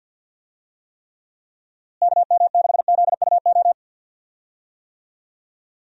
Morse code, heard as a keyed tone, about 52 wpm, 700 Hz. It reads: KM6ZUO
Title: USA Call Signs - 6 Characters - Repeated 2x - 50wpm